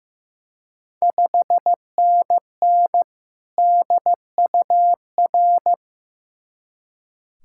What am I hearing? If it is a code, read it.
Morse code, heard as a keyed tone, about 15 wpm, 700 Hz. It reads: 5NN DUR
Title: Ontario QSO Party Exchange 15wpm